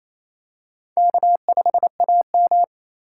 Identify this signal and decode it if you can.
Morse code, heard as a keyed tone, about 27 wpm, 700 Hz. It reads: K5AM
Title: K1USN SST QSO Practice - 28wpm